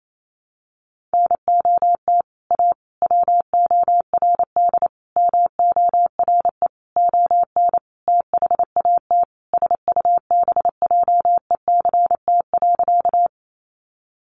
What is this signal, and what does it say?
Morse code, heard as a keyed tone, about 28 wpm, 700 Hz. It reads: NOT A WORD MORE ON THAT SUBJECT.
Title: Sentences from Top 800 Words 28wpm